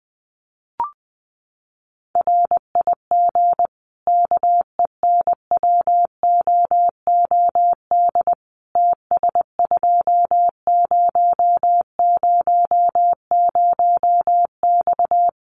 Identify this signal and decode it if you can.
Morse code, heard as a keyed tone, about 20 wpm, 700 Hz. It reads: RIG KENWOOD TS2000X
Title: Common QSO Phrases 20wpm